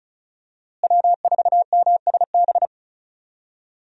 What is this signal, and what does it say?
Morse code, heard as a keyed tone, about 34 wpm, 700 Hz. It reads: W4MSB
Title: USA Call Signs - 5 Characters - Repeated 2x - 35wpm